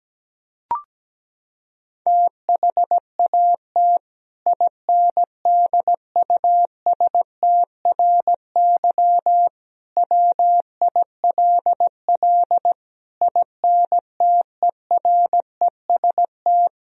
Morse code, heard as a keyed tone, about 17 wpm, 700 Hz. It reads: THAT INDUSTRY WILL INTEREST
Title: Sentences from Top 1000 Words - Review Part 4 - 17wpm